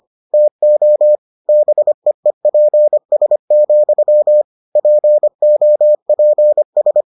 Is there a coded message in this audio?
Yes: Morse code from 0.3 to 7.0 s, heard as a keyed tone, about 25 wpm, 600 Hz. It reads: TO BEEPS, POPS